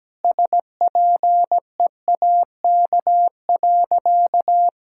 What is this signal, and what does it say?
Morse code, heard as a keyed tone, about 17 wpm, 700 Hz. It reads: SPEAK.
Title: Sentences from Top 1600 Words 17wpm